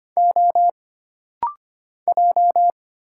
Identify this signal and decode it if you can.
Morse code, heard as a keyed tone, about 25 wpm, 700 Hz. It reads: O J
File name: Single Character (Letter:Number:Punctuation:Prosign) - Rapid-Fire - 25wpm